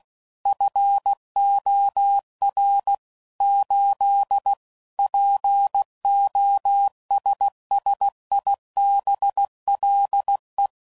Morse code, heard as a keyed tone, about 16 wpm, 800 Hz. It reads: FOR 8 POSSIBLE
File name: CW Code Groups @15WPM / Technical Subjects / AND Gate